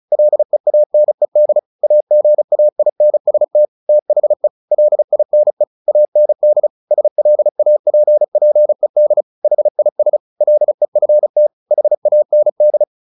Morse code, heard as a keyed tone, about 34 wpm, 600 Hz. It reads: LEANED AGAINST THE LINE AND SLAPPED HIS LEFT HAND
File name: the old man and the sea / the old man and the sea 1 35wpm